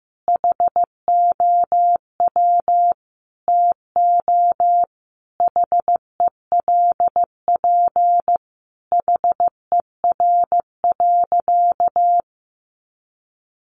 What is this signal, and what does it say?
Morse code, heard as a keyed tone, about 15 wpm, 700 Hz. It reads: HOW TO HELP HER.
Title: Sentences from Top 1000 Words 15wpm